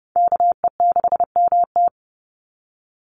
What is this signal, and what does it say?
Morse code, heard as a keyed tone, about 30 wpm, 700 Hz. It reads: KE6MT
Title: NA SOTA Chaser Callsigns 30wpm